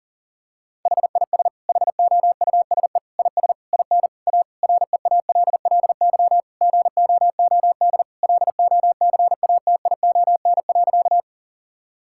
Morse code, heard as a keyed tone, about 40 wpm, 700 Hz. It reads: HIS HOUSE IS IN A REALLY GOOD LOCATION.
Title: Sentences from Top 1000 Words - Review Part 6 - 40wpm